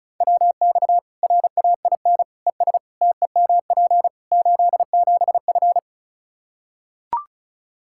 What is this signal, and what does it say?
Morse code, heard as a keyed tone, about 36 wpm, 700 Hz. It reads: WX RAIN ES TEMP 87F
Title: Practice QSO - Exchange 2 - Info - 35wpm